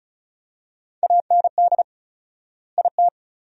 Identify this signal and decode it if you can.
Morse code, heard as a keyed tone, about 36 wpm, 700 Hz. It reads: AND IT
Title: 4x Word-Spacing - Sentences from Top 200 Words - 35wpm